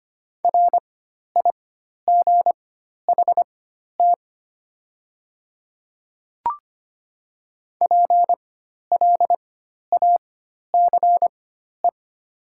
Morse code, heard as a keyed tone, about 25 wpm, spaced as for 12 wpm, 700 Hz. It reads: RIGHT PLACE
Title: 4x Character Spacing - Top 200 Words - 25wpm